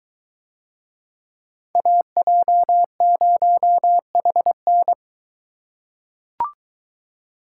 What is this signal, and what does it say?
Morse code, heard as a keyed tone, about 22 wpm, 700 Hz. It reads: AJ0HN